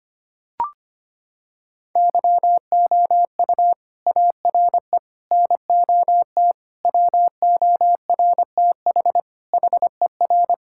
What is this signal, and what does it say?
Morse code, heard as a keyed tone, about 25 wpm, 700 Hz. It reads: YOU ARE NOT WORTH HER
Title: Sentences from Top 1600 Words 25wpm